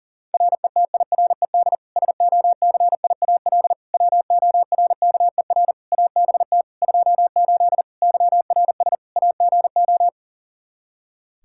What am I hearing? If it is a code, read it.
Morse code, heard as a keyed tone, about 40 wpm, 700 Hz. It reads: RETIRED SOCIAL WORKER ABT 28 YRS AGO
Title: K1USN SST QSO Practice 40wpm